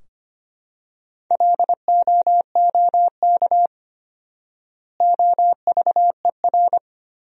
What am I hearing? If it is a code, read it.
Morse code, heard as a keyed tone, about 25 wpm, 700 Hz. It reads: LOOK OVER
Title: 4x Word-Spacing - Sentences from Top 100 Words - 25wpm